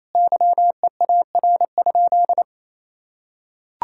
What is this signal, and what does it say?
Morse code, heard as a keyed tone, about 27 wpm, 700 Hz. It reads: YEAR?
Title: Sentences from Top 1000 Words - Review Part 5 - 28wpm